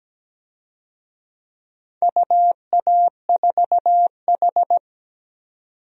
Morse code, heard as a keyed tone, about 17 wpm, 700 Hz. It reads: UA4H
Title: DXCC-100 Call Signs - 4 Characters - 17wpm